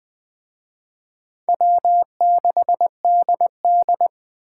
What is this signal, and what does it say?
Morse code, heard as a keyed tone, about 19 wpm, 700 Hz. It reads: W6DD